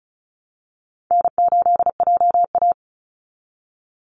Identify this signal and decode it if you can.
Morse code, heard as a keyed tone, about 34 wpm, 700 Hz. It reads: N8JA
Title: Call Signs Repeated 2x - 35wpm